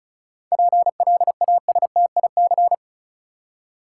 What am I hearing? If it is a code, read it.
Morse code, heard as a keyed tone, about 36 wpm, 700 Hz. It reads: PLASTIC
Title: Seven Letter Words - 35wpm